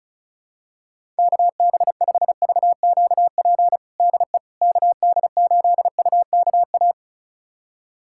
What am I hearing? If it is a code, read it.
Morse code, heard as a keyed tone, about 36 wpm, 700 Hz. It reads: KB5VQP DE KD8UKA
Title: USA Callsign DE Callsign - 6 Characters - 35wpm